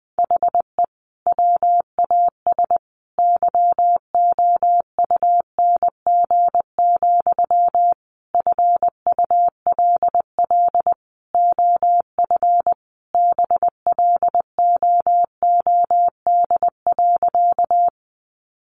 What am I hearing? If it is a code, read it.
Morse code, heard as a keyed tone, about 20 wpm, 700 Hz. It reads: HE WAS YOUNG, FULL OF BLOOD.
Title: Sentences from Top 1100 Words 20wpm